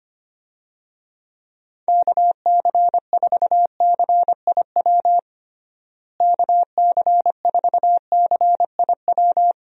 Morse code, heard as a keyed tone, about 25 wpm, 700 Hz. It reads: KC4CIW KC4CIW
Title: USA Call Signs - 6 Characters - Repeated 2x - 25wpm